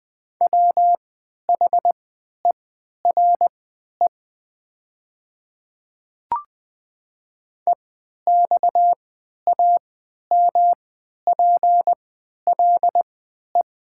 Morse code, heard as a keyed tone, about 20 wpm, spaced as for 11 wpm, 700 Hz. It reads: WHERE EXAMPLE
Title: 3x Character Spacing - Top 300 Words - 20wpm